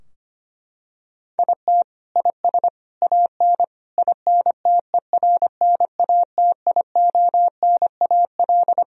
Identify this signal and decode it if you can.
Morse code, heard as a keyed tone, about 25 wpm, 700 Hz. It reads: IT IS AN INTERNATIONAL
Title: Sentences from Top 1000 Words - Review Part 4 - 25wpm